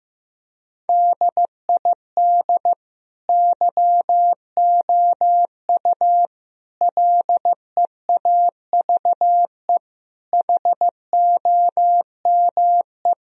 Morse code, heard as a keyed tone, about 15 wpm, 700 Hz. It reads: DID YOU LEAVE HOME
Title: Sentences from Top 1000 Words - Review Part 4 - 15wpm